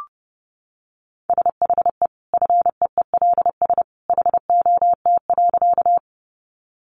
Morse code, heard as a keyed tone, about 30 wpm, 700 Hz. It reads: SHE FEELS HOT.
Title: Easy Phrases 30wpm